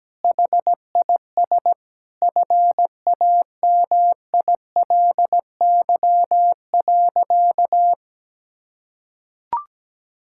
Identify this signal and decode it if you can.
Morse code, heard as a keyed tone, about 17 wpm, 700 Hz. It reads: HIS FAMILY.
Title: Sentences from Top 1000 Words - Review Part 5 - 17wpm